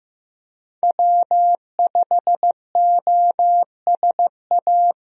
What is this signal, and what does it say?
Morse code, heard as a keyed tone, about 15 wpm, 700 Hz. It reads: W5OSA
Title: USA Call Signs - 5 Characters - Repeated 2x - 15wpm